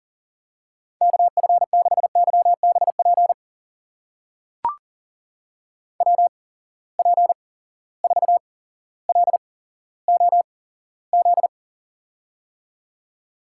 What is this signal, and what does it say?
Morse code, heard as a keyed tone, about 42 wpm, spaced as for 17 wpm, 700 Hz. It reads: KF6YBP WP4LOZ
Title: USA Call Signs - 6 Characters - 8x Character Spacing - 40wpm